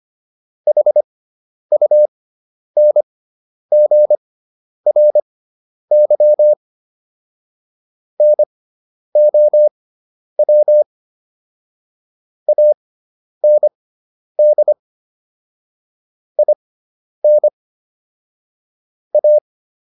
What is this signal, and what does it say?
Morse code, heard as a keyed tone, about 25 wpm, spaced as for 10 wpm, 600 Hz. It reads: HUNGRY NOW AND IN A